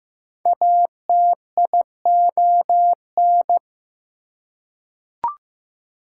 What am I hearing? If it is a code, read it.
Morse code, heard as a keyed tone, about 15 wpm, 700 Hz. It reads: ATION
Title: Prefixes and Suffixes 15wpm